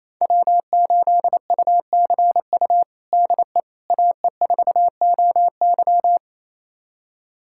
Morse code, heard as a keyed tone, about 28 wpm, 700 Hz. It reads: W8UCU DE AE4OY
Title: USA Callsign DE Callsign 28wpm